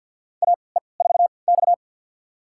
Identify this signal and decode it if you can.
Morse code, heard as a keyed tone, about 53 wpm, spaced as for 29 wpm, 700 Hz. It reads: AE4X
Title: USA Call Signs - 4 Characters - 3x Character Spacing - 50wpm